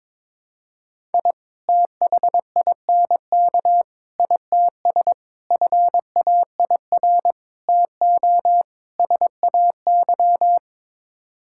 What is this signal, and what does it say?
Morse code, heard as a keyed tone, about 22 wpm, 700 Hz. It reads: I THINK ITS FAIR TO SAY
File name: practice / 22wpm / Sets of 6 Words - 22wpm